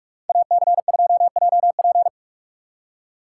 Morse code, heard as a keyed tone, about 64 wpm, spaced as for 54 wpm, 700 Hz. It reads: AC2JP
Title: Speed Racing - USA 2x2 Call Signs 45wpm